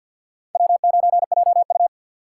Morse code, heard as a keyed tone, about 47 wpm, 700 Hz. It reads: W9JU
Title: USA Call Signs - 4 Characters - 2x Character Spacing - 50wpm